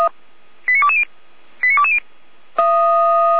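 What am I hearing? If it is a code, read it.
Five-tone selective call (ZVEI), heard as quick runs of five tones, tones of about 70 ms. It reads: 89220 89220
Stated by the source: SDR-received ZVEI five-tone call 89223, from the audio server of the Orlamünde fire brigade (feuerwehr-orlamuende).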